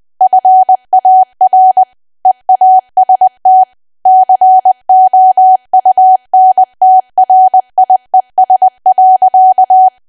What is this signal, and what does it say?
Morse code, heard as a keyed tone, about 20 wpm, 750 Hz. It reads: FAR EAST COUNTRIES.
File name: cwa / W1AW-Level3 / 20WPM